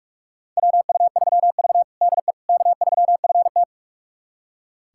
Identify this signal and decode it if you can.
Morse code, heard as a keyed tone, about 42 wpm, 700 Hz. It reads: WU3V DE K3FT
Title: USA Callsign DE Callsign - 4 Characters - 45wpm